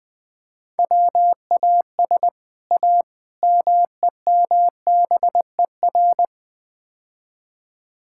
Morse code, heard as a keyed tone, about 20 wpm, 700 Hz. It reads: WAS A MEMBER
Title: Build A Phrase - 5-Word Phrases - 20wpm